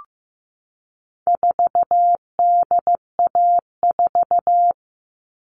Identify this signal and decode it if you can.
Morse code, heard as a keyed tone, about 15 wpm, 700 Hz. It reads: 4DA4
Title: All Cut Numbers - 4-Digit 15wpm